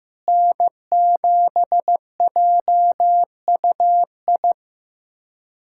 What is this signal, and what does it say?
Morse code, heard as a keyed tone, about 15 wpm, 700 Hz. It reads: N7JUI
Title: USA Callsign DE Callsign - 5 Characters - 15wpm